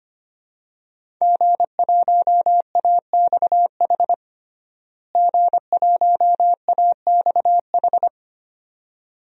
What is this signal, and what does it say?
Morse code, heard as a keyed tone, about 25 wpm, 700 Hz. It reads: G1AXH G1AXH
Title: DXCC-100 Call Signs - 5 Characters - Repeated 2x - 25wpm